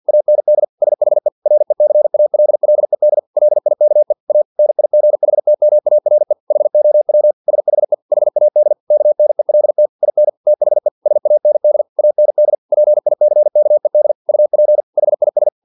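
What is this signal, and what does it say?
Morse code, heard as a keyed tone, about 50 wpm, 600 Hz. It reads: AND SHE RECALLED LIKE A NIGHTMARE HOW SHE HAD KNELT IN THE SAND AND PICKED UP HIS
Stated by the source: Morse practice text from an English-language novel at 49 wpm.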